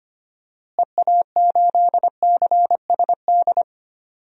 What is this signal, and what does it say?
Morse code, heard as a keyed tone, about 25 wpm, 700 Hz. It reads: EA8CSD
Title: DXCC-100 Call Signs - 6 Characters - Repeated 3x - 25wpm